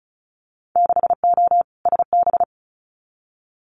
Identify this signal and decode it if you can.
Morse code, heard as a keyed tone, about 36 wpm, 700 Hz. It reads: 6O SB
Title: Winter Field Day Exchanges Encore 35wpm